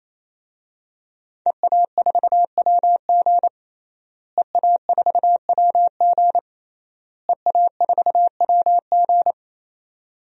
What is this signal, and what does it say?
Morse code, heard as a keyed tone, about 28 wpm, 700 Hz. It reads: EA4WG EA4WG EA4WG
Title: DXCC-100 Call Signs - 5 Characters - Repeated 3x - 28wpm